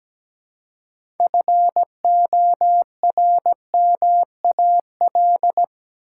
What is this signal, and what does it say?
Morse code, heard as a keyed tone, about 17 wpm, 700 Hz. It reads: FORMAL
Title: Six Letter Words - 17wpm